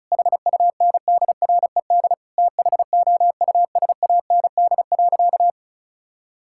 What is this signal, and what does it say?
Morse code, heard as a keyed tone, about 36 wpm, 700 Hz. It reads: HUNDRED THOUSAND.
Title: Sentences from Top 1000 Words - Review Part 5 - 35wpm